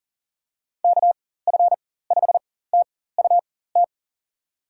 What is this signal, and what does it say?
Morse code, heard as a keyed tone, about 39 wpm, spaced as for 18 wpm, 700 Hz. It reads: KF5TUT
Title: USA Call Signs - 6 Characters - 4x Character Spacing - 40wpm